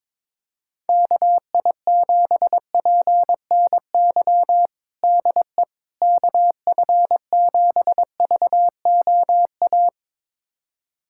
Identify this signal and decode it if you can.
Morse code, heard as a keyed tone, about 22 wpm, 700 Hz. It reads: KI7PNY DE KF7VOA
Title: USA Callsign DE Callsign - 6 Characters - 22wpm